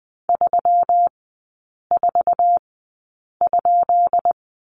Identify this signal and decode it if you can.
Morse code, heard as a keyed tone, about 20 wpm, spaced as for 8 wpm, 700 Hz. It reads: 34?
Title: 178 Lesson 29e - Slash - Three Characters 20wpm